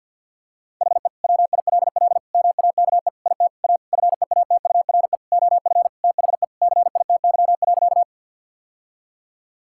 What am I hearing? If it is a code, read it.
Morse code, heard as a keyed tone, about 50 wpm, 700 Hz. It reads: HE WILL MAKE IT A FEATURE OF THE CITY.
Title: Sentences from Top 1000 Words - Review Part 5 - 50wpm